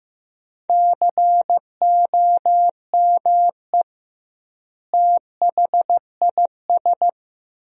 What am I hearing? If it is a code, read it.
Morse code, heard as a keyed tone, about 15 wpm, 700 Hz. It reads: COME THIS
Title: 2x Word-Spacing - Sentences from Top 100 Words - 15wpm